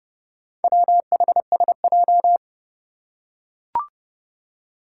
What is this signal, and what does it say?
Morse code, heard as a keyed tone, about 30 wpm, 700 Hz. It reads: WHSJ
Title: Four Challenging Letter-Number Combinations 30wpm